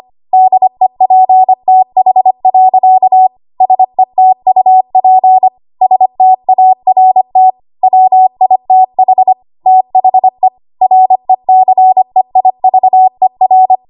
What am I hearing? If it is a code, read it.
Morse code, heard as a keyed tone, about 25 wpm, 750 Hz. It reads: DEPTH. SETUP START WITH THE RECEIVER
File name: Archive / 25 WPM / 25WPM